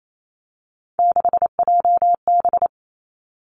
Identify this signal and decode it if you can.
Morse code, heard as a keyed tone, about 28 wpm, 700 Hz. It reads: 6JB
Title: Three Challenging Letter-Number Combinations 28wpm